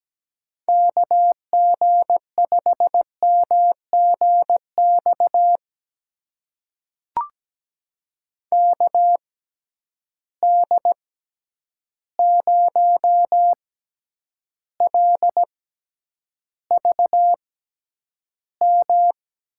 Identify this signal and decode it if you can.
Morse code, heard as a keyed tone, about 17 wpm, spaced as for 9 wpm, 700 Hz. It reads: KG5MGX KD0LVM
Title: USA Call Signs - 6 Characters - 6x Character Spacing - 17wpm